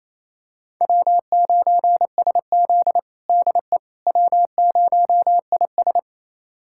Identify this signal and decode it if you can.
Morse code, heard as a keyed tone, about 28 wpm, 700 Hz. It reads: W9SZ DE W0IS
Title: USA Callsign DE Callsign - 4 Characters - 28wpm